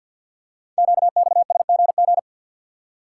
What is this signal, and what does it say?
Morse code, heard as a keyed tone, about 46 wpm, 700 Hz. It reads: QXSGG
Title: Code Groups - 5 Letter:Numbers - 50wpm